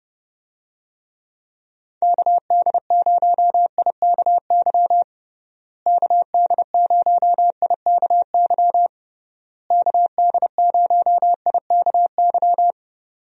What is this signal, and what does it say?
Morse code, heard as a keyed tone, about 30 wpm, 700 Hz. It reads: KD0IKY KD0IKY KD0IKY
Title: USA Call Signs - 6 Characters - Repeated 3x - 30wpm